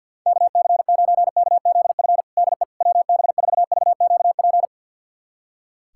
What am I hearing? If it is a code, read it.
Morse code, heard as a keyed tone, about 52 wpm, 700 Hz. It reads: KC9KZF DE WB4VQP
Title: USA Callsign DE Callsign - 6 Characters - 50wpm